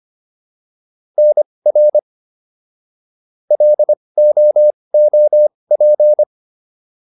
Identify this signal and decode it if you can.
Morse code, heard as a keyed tone, about 25 wpm, spaced as for 20 wpm, 600 Hz. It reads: NR LOOP